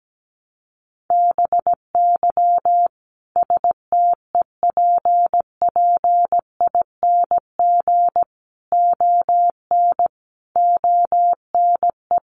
Morse code, heard as a keyed tone, about 17 wpm, 700 Hz. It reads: BY STEPPING ON ONE
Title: A Sound Of Thunder 17wpm